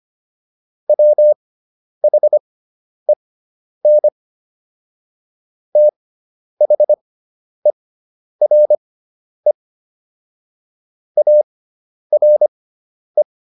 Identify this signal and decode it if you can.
Morse code, heard as a keyed tone, about 25 wpm, spaced as for 10 wpm, 600 Hz. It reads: WHEN THERE ARE